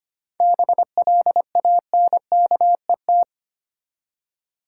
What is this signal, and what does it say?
Morse code, heard as a keyed tone, about 25 wpm, 690 Hz. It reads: BLANKET
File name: practice / 25wpm / Easy Words with Hints - 25wpm